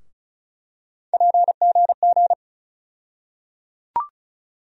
Morse code, heard as a keyed tone, about 36 wpm, 700 Hz. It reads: PGG
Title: USA Call Sign Postfixes - 3 Letters 35wpm